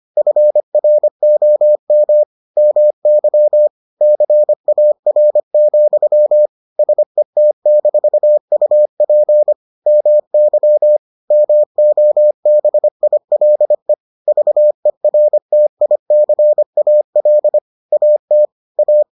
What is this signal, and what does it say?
Morse code, heard as a keyed tone, about 25 wpm, 600 Hz. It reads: FROM MY CAR, SET-UP MY MOBILE VERTICAL AT A